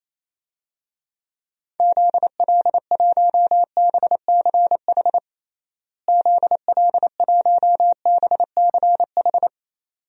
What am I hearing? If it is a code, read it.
Morse code, heard as a keyed tone, about 28 wpm, 700 Hz. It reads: ZL1BCH ZL1BCH
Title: DXCC-100 Call Signs - 6 Characters - Repeated 2x - 28wpm